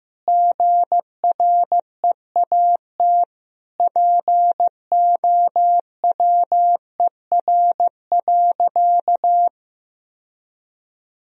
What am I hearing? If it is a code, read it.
Morse code, heard as a keyed tone, about 15 wpm, 700 Hz. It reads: GREAT POWER.
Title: Sentences from Top 1000 Words - Review Part 4 - 15wpm